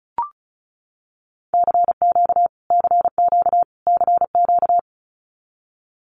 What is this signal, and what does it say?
Morse code, heard as a keyed tone, about 34 wpm, 700 Hz. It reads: CQ CQ CQ
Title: Abbreviations Repeated 3x - 35wpm